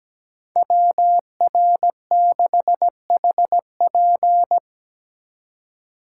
Morse code, heard as a keyed tone, about 17 wpm, 700 Hz. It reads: WR6HP
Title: USA Call Signs - 5 Characters - 3x Character Spacing - 17wpm